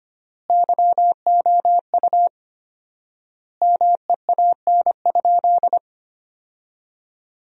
Morse code, heard as a keyed tone, about 25 wpm, 700 Hz. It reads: YOU MEAN?
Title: 4x Word-Spacing - Sentences from Top 500 Words - 25wpm